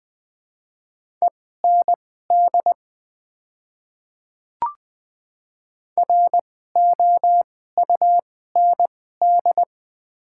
Morse code, heard as a keyed tone, about 20 wpm, spaced as for 15 wpm, 700 Hz. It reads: END ROUND
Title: 2x Character Spacing - Top 200 Words - 20wpm